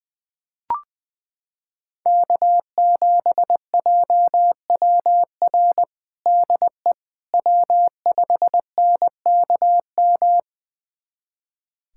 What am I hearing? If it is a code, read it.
Morse code, heard as a keyed tone, about 20 wpm, 700 Hz. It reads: K7JWR DE W5NKM
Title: USA Callsign DE Callsign 20wpm